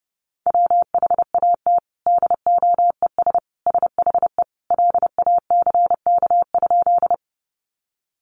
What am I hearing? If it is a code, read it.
Morse code, heard as a keyed tone, about 30 wpm, 700 Hz. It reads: WHAT DOES SHE LACK?
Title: Sentences from Top 1100 Words 30wpm